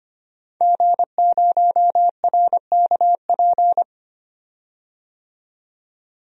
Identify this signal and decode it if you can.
Morse code, heard as a keyed tone, about 25 wpm, 700 Hz. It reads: G0RKP
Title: DXCC-100 Call Signs - 5 Characters - 25wpm